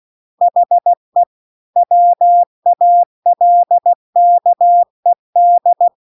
Morse code, heard as a keyed tone, about 16 wpm, 700 Hz. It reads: HE WALKED